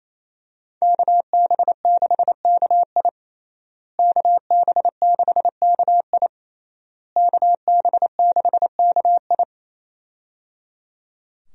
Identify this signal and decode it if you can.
Morse code, heard as a keyed tone, about 28 wpm, 700 Hz. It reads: KB6KI KB6KI KB6KI
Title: USA Call Signs - 5 Characters - Repeated 3x - 28wpm